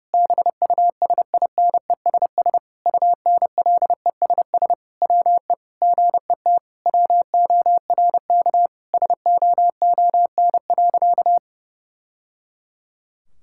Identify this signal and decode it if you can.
Morse code, heard as a keyed tone, about 30 wpm, 700 Hz. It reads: BUSINESS UNLESS WE GET WORK SOON.
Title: Sentences from Top 1000 Words - Review Part 6 - 30wpm